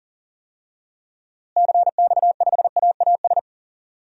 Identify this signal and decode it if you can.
Morse code, heard as a keyed tone, about 40 wpm, 700 Hz. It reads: CX5AAS
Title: DXCC-100 Call Signs - 6 Characters - Repeated 3x - 40wpm